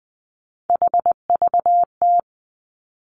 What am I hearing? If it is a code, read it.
Morse code, heard as a keyed tone, about 20 wpm, 700 Hz. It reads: HVT
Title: Three Challenging Letter-Number Combinations 20wpm